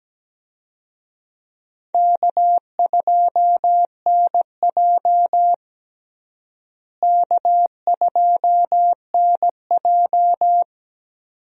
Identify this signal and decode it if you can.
Morse code, heard as a keyed tone, about 17 wpm, 700 Hz. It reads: K2NJ K2NJ
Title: USA Call Signs - 4 Characters - Repeated 2x - 17wpm